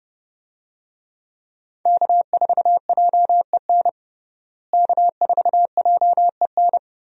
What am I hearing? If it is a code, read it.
Morse code, heard as a keyed tone, about 30 wpm, 700 Hz. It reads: K4JEN K4JEN
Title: USA Call Signs - 5 Characters - Repeated 3x - 30wpm